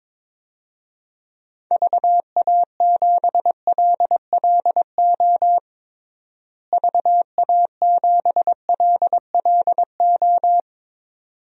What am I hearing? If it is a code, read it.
Morse code, heard as a keyed tone, about 21 wpm, 700 Hz. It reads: VA7LLO VA7LLO
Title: DXCC-100 Call Signs - 6 Characters - Repeated 2x - 22wpm